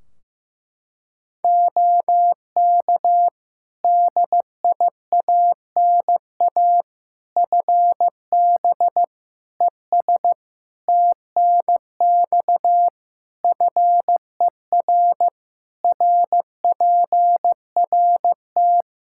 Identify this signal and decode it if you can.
Morse code, heard as a keyed tone, about 15 wpm, 700 Hz. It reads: OK DIANA FB ES TNX FER RPRT
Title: Practice QSO - Exchange 2 - Info - 15wpm